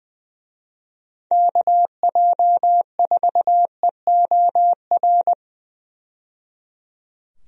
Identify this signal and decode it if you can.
Morse code, heard as a keyed tone, about 20 wpm, 700 Hz. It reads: KJ4EOR
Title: USA Call Signs - 6 Characters - 20wpm